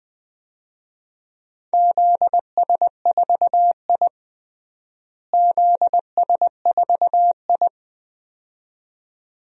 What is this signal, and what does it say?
Morse code, heard as a keyed tone, about 20 wpm, 700 Hz. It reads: ZS4I ZS4I
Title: DXCC-100 Call Signs - 4 Characters - Repeated 2x - 20wpm